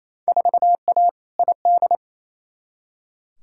Morse code, heard as a keyed tone, about 28 wpm, 700 Hz. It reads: 4A ID
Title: Field Day Exchanges Encore - 28wpm